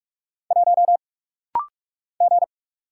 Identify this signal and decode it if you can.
Morse code, heard as a keyed tone, about 44 wpm, 700 Hz. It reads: 1 G
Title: Single Letter-Number Alt - Rapid-Fire 45wpm